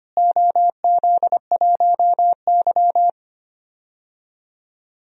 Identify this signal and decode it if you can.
Morse code, heard as a keyed tone, about 25 wpm, 700 Hz. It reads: OZ1Y
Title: DXCC-100 Call Signs - 4 Characters - Repeated 3x - 25wpm